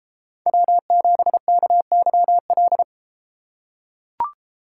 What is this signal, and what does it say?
Morse code, heard as a keyed tone, about 32 wpm, 700 Hz. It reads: W7KYL